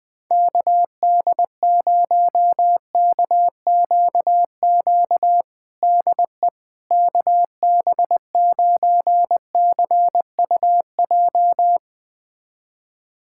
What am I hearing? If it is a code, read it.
Morse code, heard as a keyed tone, about 21 wpm, 700 Hz. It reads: KD0KQQ DE KB9CUJ
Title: USA Callsign DE Callsign - 6 Characters - 20wpm